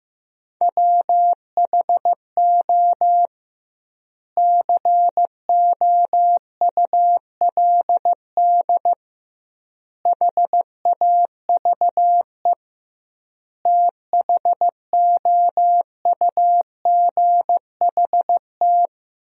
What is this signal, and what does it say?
Morse code, heard as a keyed tone, about 15 wpm, 700 Hz. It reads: WHO COULD HAVE THOUGHT
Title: 2x Word-Spacing - Sentences from Top 300 Words - 15wpm